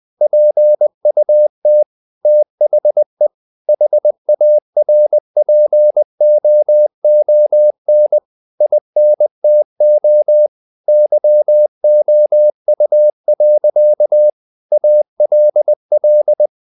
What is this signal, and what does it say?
Morse code, heard as a keyed tone, about 20 wpm, 600 Hz. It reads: PUT THE HARPOON INTO YOU. ALL
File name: the old man and the sea / the old man and the sea part 1 20wpm